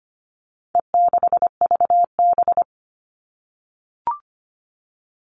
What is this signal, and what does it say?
Morse code, heard as a keyed tone, about 25 wpm, 700 Hz. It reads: E6VB
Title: All Cut Numbers - 4-Digit 25wpm